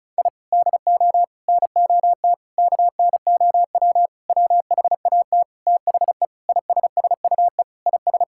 Morse code, heard as a keyed tone, about 36 wpm, 700 Hz. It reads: I DO NOT KNOW WHAT THE ISSUE IS
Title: Sentences from Top 1000 Words - Review Part 6 - 35wpm